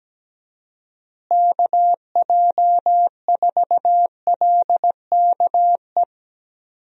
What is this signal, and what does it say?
Morse code, heard as a keyed tone, about 17 wpm, 700 Hz. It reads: KJ4LKE
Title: USA Call Signs - 6 Characters - Repeated 2x - 17wpm